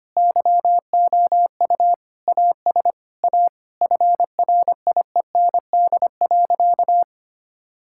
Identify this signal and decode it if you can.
Morse code, heard as a keyed tone, about 25 wpm, 700 Hz. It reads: YOU AS A FRIEND.
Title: Sentences from Top 1000 Words - Review Part 6 - 25wpm